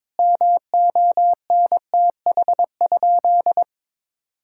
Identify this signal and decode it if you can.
Morse code, heard as a keyed tone, about 22 wpm, 700 Hz. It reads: MONTH?
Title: Sentences from Top 1000 Words - Review Part 6 - 22wpm